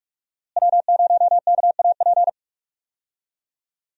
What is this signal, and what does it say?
Morse code, heard as a keyed tone, about 46 wpm, 700 Hz. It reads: W0KAP
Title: USA Call Signs - 5 Characters - Repeated 3x - 45wpm